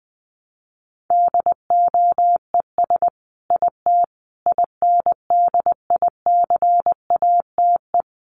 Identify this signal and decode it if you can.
Morse code, heard as a keyed tone, about 20 wpm, 700 Hz. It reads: DOES IT INDICATE
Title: Sentences from Top 1300 Words 20wpm